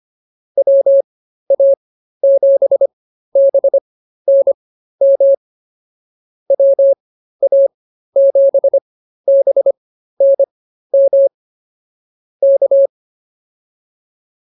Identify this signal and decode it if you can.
Morse code, heard as a keyed tone, about 24 wpm, spaced as for 13 wpm, 550 Hz. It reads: WA7BNM WA7BNM K